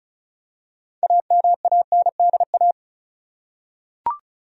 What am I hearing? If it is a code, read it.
Morse code, heard as a keyed tone, about 36 wpm, 700 Hz. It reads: AMANDA
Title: USA Common Names - 35wpm